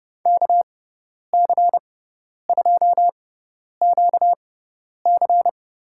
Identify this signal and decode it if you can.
Morse code, heard as a keyed tone, about 30 wpm, spaced as for 10 wpm, 700 Hz. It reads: KC2QC
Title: USA Call Signs - 6x Character Spacing - 30wpm